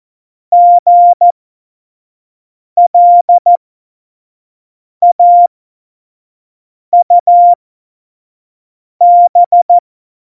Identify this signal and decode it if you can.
Morse code, heard as a keyed tone, about 14 wpm, spaced as for 5 wpm, 700 Hz. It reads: GLAUB
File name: Morsekurs / Witze / Witz 02. Oktober 2018, 25 F70